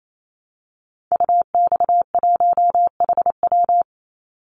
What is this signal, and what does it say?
Morse code, heard as a keyed tone, about 28 wpm, 700 Hz. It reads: UX1HW
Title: Call Signs Repeated 3x - 28wpm